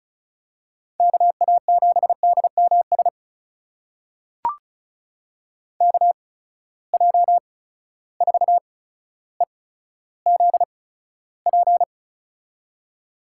Morse code, heard as a keyed tone, about 36 wpm, spaced as for 15 wpm, 700 Hz. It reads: KA7DMS KJ4EZP
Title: USA Call Signs - 6 Characters - 8x Character Spacing - 35wpm